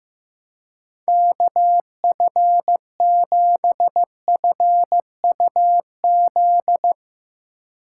Morse code, heard as a keyed tone, about 15 wpm, 700 Hz. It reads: KF7FUZ
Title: USA Call Signs - 6 Characters - 3x Character Spacing - 15wpm